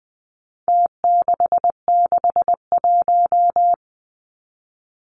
Morse code, heard as a keyed tone, about 20 wpm, 700 Hz. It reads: T661
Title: Numbers - Contest Style - 4-Digit 20wpm